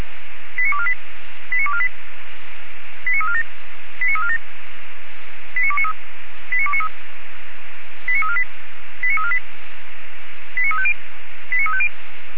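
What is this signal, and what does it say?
Five-tone selective call (ZVEI), heard as quick runs of five tones, tones of about 70 ms. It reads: 89248 89248 89347 89347 89293 89293 89348 89348 89350 89350
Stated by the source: Received on SDR, ZVEI five-tone code 89461, 464 channel.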